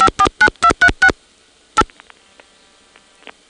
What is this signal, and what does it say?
DTMF, heard as telephone touch tones, tones of about 80 ms. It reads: 549236